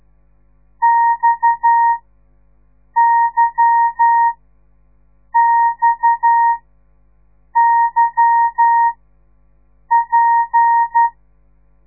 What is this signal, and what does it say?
Morse code, heard as a keyed tone, about 12 wpm, spaced as for 6 wpm, 930 Hz. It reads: XYXYP
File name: radiomorningstar / morse / les 8